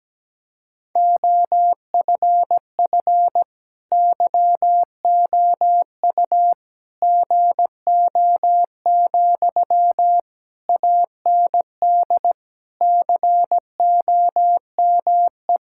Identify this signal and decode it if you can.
Morse code, heard as a keyed tone, about 17 wpm, 700 Hz. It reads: OFF YOU GO, AND COME
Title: Sentences from Top 1000 Words - Review Part 6 - 17wpm